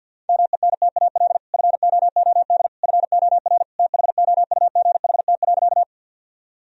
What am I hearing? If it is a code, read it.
Morse code, heard as a keyed tone, about 50 wpm, 700 Hz. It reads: MENTAL FOOD FOR THOUGHT.
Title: Sentences from Top 1000 Words - Review Part 4 - 50wpm